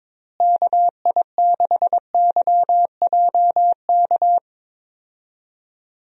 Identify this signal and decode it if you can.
Morse code, heard as a keyed tone, about 22 wpm, 700 Hz. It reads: KI6YJK
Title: USA Call Signs - 6 Characters - Repeated 3x - 22wpm